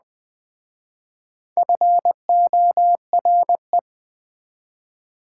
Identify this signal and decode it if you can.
Morse code, heard as a keyed tone, about 24 wpm, 700 Hz. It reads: FORE
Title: Speed-Racing Prefixes and Suffixes 20wpm